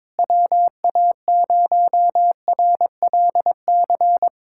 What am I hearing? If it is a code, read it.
Morse code, heard as a keyed tone, about 22 wpm, 700 Hz. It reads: WA0RLC
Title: USA Callsign DE Callsign 22wpm